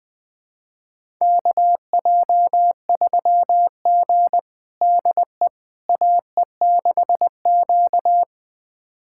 Morse code, heard as a keyed tone, about 21 wpm, 700 Hz. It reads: KJ3G DE AE6Q
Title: USA Callsign DE Callsign - 4 Characters - 20wpm